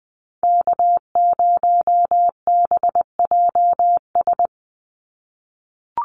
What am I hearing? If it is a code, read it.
Morse code, heard as a keyed tone, about 20 wpm, 700 Hz. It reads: K0BJS
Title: USA Call Signs 20wpm